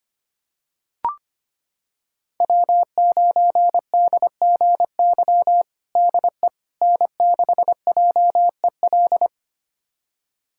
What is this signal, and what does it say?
Morse code, heard as a keyed tone, about 25 wpm, 700 Hz. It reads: W9DGY DE N6JEL
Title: USA Callsign DE Callsign - 5 Characters - 25wpm